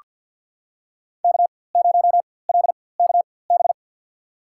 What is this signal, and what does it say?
Morse code, heard as a keyed tone, about 47 wpm, spaced as for 23 wpm, 700 Hz. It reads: K0LKB
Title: USA Call Signs - 4x Character Spacing - 50wpm